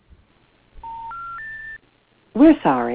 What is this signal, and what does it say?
Special information tone.